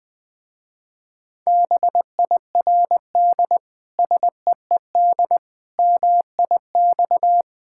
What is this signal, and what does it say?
Morse code, heard as a keyed tone, about 20 wpm, 700 Hz. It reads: BIRD SEED MIX
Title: Difficult Words with Hints - 20wpm